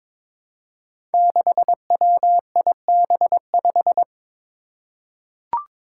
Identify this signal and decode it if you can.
Morse code, heard as a keyed tone, about 22 wpm, 700 Hz. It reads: 6WIB5
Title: Five Challenging Letter-Number Combinations 22wpm